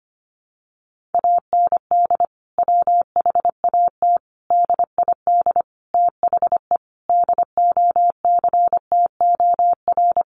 Morse code, heard as a keyed tone, about 25 wpm, 700 Hz. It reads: AND WHAT DID THE DOCTOR
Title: Sentences from Top 1500 Words 25wpm